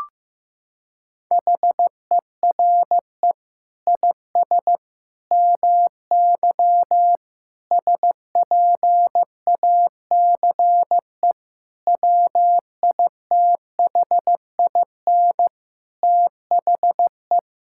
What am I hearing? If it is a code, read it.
Morse code, heard as a keyed tone, about 15 wpm, 700 Hz. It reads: HERE IS MY SPACE WITHIN THE
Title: Sentences from Top 1000 Words - Review Part 5 - 15wpm